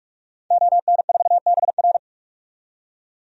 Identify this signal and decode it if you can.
Morse code, heard as a keyed tone, about 46 wpm, 700 Hz. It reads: ON4BR
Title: DXCC-100 Call Signs - 5 Characters - 45wpm